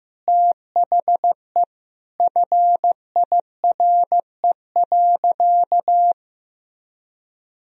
Morse code, heard as a keyed tone, about 15 wpm, 700 Hz. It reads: THE FIRE.
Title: Sentences from Top 1000 Words - Review Part 2 - 15wpm